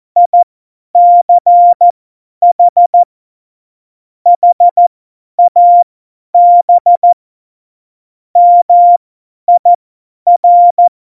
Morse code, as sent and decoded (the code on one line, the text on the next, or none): .. -.-. .... / .... .- -... / -- .. .-.
ICH HAB MIR